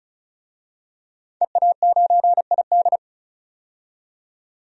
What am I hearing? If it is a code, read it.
Morse code, heard as a keyed tone, about 36 wpm, 700 Hz. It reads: EA9ID
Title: DXCC-100 Call Signs - 5 Characters - 35wpm